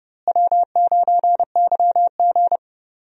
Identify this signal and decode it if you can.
Morse code, heard as a keyed tone, about 30 wpm, 700 Hz. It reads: W9YG
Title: USA Call Signs - 4 Characters - Repeated 2x - 30wpm